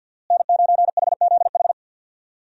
Morse code, heard as a keyed tone, about 52 wpm, 700 Hz. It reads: N9HZH
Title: USA Call Signs - 5 Characters - Repeated 3x - 50wpm